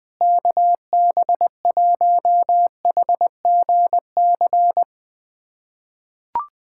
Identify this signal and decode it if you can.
Morse code, heard as a keyed tone, about 21 wpm, 700 Hz. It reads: KB1HGC
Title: USA Callsign DE Callsign 20wpm